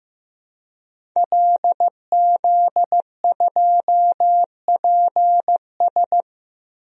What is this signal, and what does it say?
Morse code, heard as a keyed tone, about 15 wpm, 700 Hz. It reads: LZ2PS
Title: DXCC-100 Call Signs - 5 Characters - Repeated 2x - 15wpm